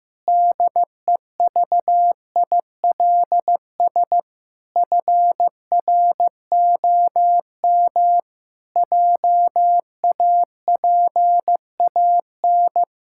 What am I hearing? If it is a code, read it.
Morse code, heard as a keyed tone, about 15 wpm, 700 Hz. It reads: DEVILS FROM JAPAN
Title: Pangram Challenge v2 - 15wpm